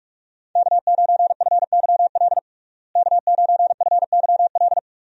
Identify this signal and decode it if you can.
Morse code, heard as a keyed tone, about 46 wpm, 700 Hz. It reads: K9FYL K9FYL
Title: USA Call Signs - 5 Characters - Repeated 2x - 45wpm